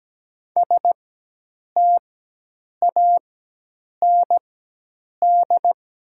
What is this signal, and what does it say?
Morse code, heard as a keyed tone, about 17 wpm, spaced as for 8 wpm, 700 Hz. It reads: STAND
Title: 4x Character Spacing - Top 500 Words - 17wpm